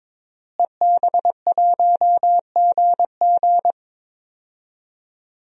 Morse code, heard as a keyed tone, about 21 wpm, 700 Hz. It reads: EB1GG
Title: DXCC-100 Call Signs - 5 Characters - Repeated 2x - 22wpm